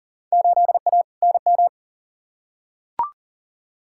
Morse code, heard as a keyed tone, about 40 wpm, 700 Hz. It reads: 8A NM
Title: Field Day Exchanges Encore - 40wpm